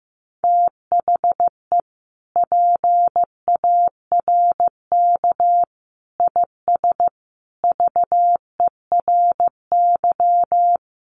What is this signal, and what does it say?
Morse code, heard as a keyed tone, about 15 wpm, 700 Hz. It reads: THE PARK IS VERY
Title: Sentences from Top 1300 Words 15wpm